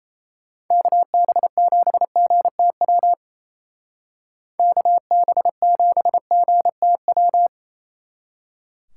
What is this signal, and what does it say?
Morse code, heard as a keyed tone, about 33 wpm, 700 Hz. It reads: KB7GTW KB7GTW
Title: Speed Racing - USA 2x3 Call Signs 28wpm